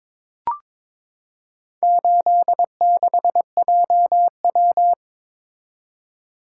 Morse code, heard as a keyed tone, about 23 wpm, 700 Hz. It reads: 86JW
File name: practice / 22wpm / Four Letter-Number Combinations - 22wpm